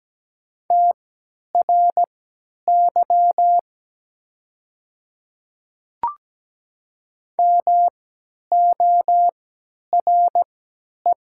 Morse code, heard as a keyed tone, about 17 wpm, spaced as for 10 wpm, 690 Hz. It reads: TRY MORE
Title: 3x Character Spacing - Top 200 Words - 17wpm